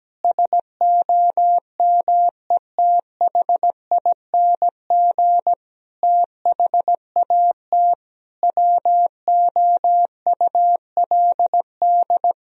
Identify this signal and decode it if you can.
Morse code, heard as a keyed tone, about 17 wpm, 700 Hz. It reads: SOMETHING THAT WOULD
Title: Sentences from Top 1000 Words - Review Part 1 - 17wpm